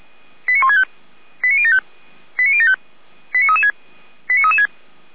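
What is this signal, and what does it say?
Five-tone selective call (ZVEI), heard as quick runs of five tones, tones of about 70 ms. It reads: 89165 89075 89075 89336 89336
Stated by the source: ZVEI five-tone call received on an SDR, from the Feuerwehr Orlamünde fire brigade audio server.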